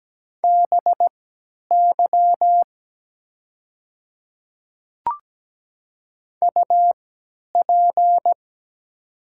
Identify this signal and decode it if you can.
Morse code, heard as a keyed tone, about 17 wpm, spaced as for 10 wpm, 700 Hz. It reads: BY UP
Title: 3x Character Spacing - Top 200 Words - 17wpm